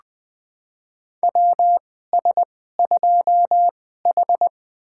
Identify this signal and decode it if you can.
Morse code, heard as a keyed tone, about 20 wpm, spaced as for 15 wpm, 700 Hz. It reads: WS2H
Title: USA Call Signs - 4 Characters - 2x Character Spacing - 20wpm